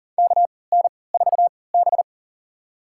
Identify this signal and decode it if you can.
Morse code, heard as a keyed tone, about 39 wpm, spaced as for 22 wpm, 700 Hz. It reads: KN4B
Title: USA Call Signs - 4 Characters - 3x Character Spacing - 40wpm